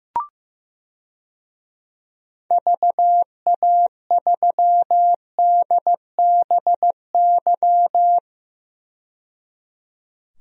Morse code, heard as a keyed tone, about 15 wpm, 690 Hz. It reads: VA3DBY
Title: DXCC-100 Call Signs - 6 Characters - 15wpm